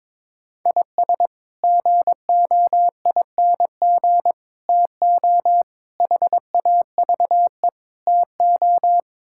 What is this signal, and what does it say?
Morse code, heard as a keyed tone, about 23 wpm, 700 Hz. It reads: IS GOING TO HAVE TO
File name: practice / 22wpm / Build A Phrase - 5-Word Phrases - 22wpm